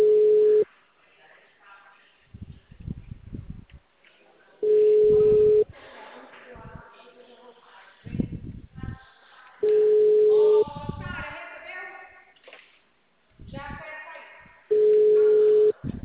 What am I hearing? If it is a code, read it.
Ringback tone (Europe, 425 Hz), with speech.
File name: out- -2016- - - .99909